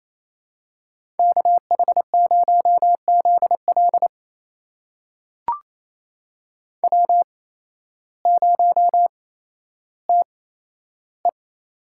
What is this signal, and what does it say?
Morse code, heard as a keyed tone, about 28 wpm, spaced as for 13 wpm, 700 Hz. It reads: KH0ZL W0TE
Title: USA Call Signs - 5 Characters - 8x Character Spacing - 28wpm